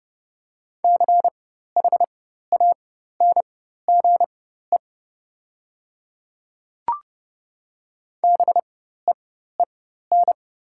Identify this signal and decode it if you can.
Morse code, heard as a keyed tone, about 30 wpm, spaced as for 14 wpm, 700 Hz. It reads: CHANGE BEEN
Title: 4x Character Spacing - Top 200 Words - 30wpm